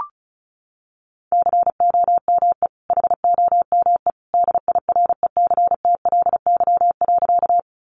Morse code, heard as a keyed tone, about 34 wpm, 700 Hz. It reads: COME HOME DIRECTLY.
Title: Sentences from Top 800 Words 35wpm